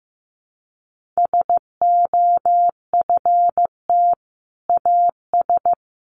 Morse code, heard as a keyed tone, about 15 wpm, 700 Hz. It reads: SOFT AS
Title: Sentences from Top 1500 Words 15wpm